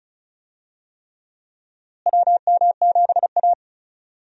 Morse code, heard as a keyed tone, about 34 wpm, 700 Hz. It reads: WM7A
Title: USA Call Signs - 4 Characters - Repeated 3x - 35wpm